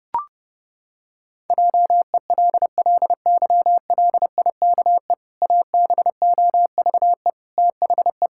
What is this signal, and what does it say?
Morse code, heard as a keyed tone, about 30 wpm, 700 Hz. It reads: JELLYLIKE ABOVE THE
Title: Pangram Challenge v2 - 30wpm